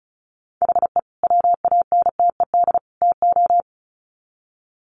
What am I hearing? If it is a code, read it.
Morse code, heard as a keyed tone, about 34 wpm, 700 Hz. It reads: HE WANTED TO
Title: Sets of 3 Words 35wpm